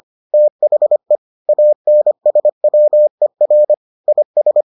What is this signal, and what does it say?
Morse code, heard as a keyed tone, about 25 wpm, 600 Hz. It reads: THE ANSWER IS